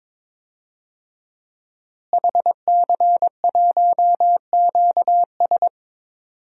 Morse code, heard as a keyed tone, about 22 wpm, 700 Hz. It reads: HC1QS
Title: DXCC-100 Call Signs - 5 Characters - Repeated 3x - 22wpm